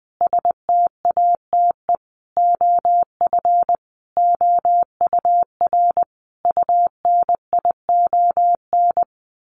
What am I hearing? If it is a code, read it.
Morse code, heard as a keyed tone, about 20 wpm, 700 Hz. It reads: STATE OF OUR UNION
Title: Sentences from Top 1400 Words 20wpm